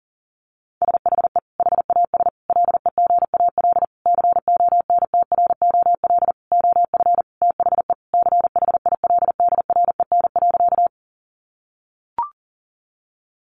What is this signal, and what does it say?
Morse code, heard as a keyed tone, about 40 wpm, 700 Hz. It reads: SHE HAS LEGAL CONTROL OF THE CHILDREN.
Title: Sentences from Top 1200 Words 40wpm